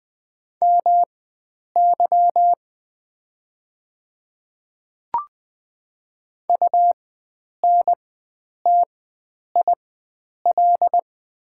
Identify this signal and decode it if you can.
Morse code, heard as a keyed tone, about 20 wpm, spaced as for 9 wpm, 700 Hz. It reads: MY UNTIL
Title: 4x Character Spacing - Top 500 Words - 20wpm